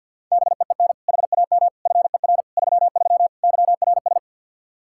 Morse code, heard as a keyed tone, about 46 wpm, 700 Hz. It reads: BEEN HAM FER 33 YRS
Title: Practice QSO - Exchange 3 - More Info - 50wpm